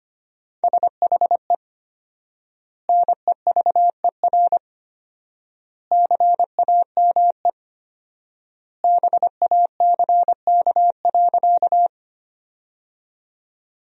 Morse code, heard as a keyed tone, about 25 wpm, 700 Hz. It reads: SHE NEVER CAME BACK.
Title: 4x Word-Spacing - Sentences from Top 300 Words - 25wpm